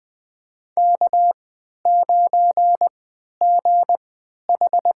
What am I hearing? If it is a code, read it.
Morse code, heard as a keyed tone, about 20 wpm, spaced as for 11 wpm, 700 Hz. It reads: K9GH
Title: USA Call Signs - 3x Character Spacing - 20wpm